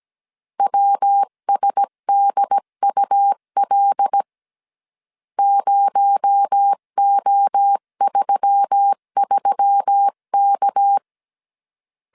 Morse code, heard as a keyed tone, about 17 wpm, 800 Hz. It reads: WSDUL 0O33K
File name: CW / DET / 1 6 1500C S C 16wpm 2733